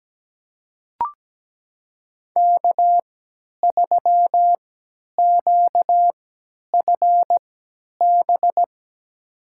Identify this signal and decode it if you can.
Morse code, heard as a keyed tone, about 17 wpm, spaced as for 10 wpm, 700 Hz. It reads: K3QFB
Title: USA Call Signs - 3x Character Spacing - 17wpm